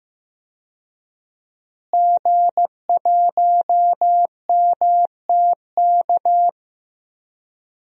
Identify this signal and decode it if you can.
Morse code, heard as a keyed tone, about 15 wpm, 700 Hz. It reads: G1MTK
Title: DXCC-100 Call Signs - 5 Characters - Repeated 2x - 15wpm